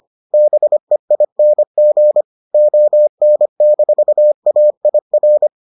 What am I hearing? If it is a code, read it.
Morse code, heard as a keyed tone, about 25 wpm, 600 Hz. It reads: BEING ON-AIR